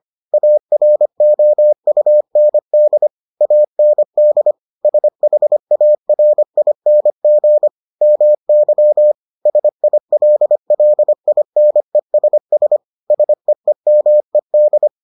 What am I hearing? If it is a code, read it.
Morse code, heard as a keyed tone, about 25 wpm, 600 Hz. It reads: AROUND AND SHARING MY SILLINESS SEEMED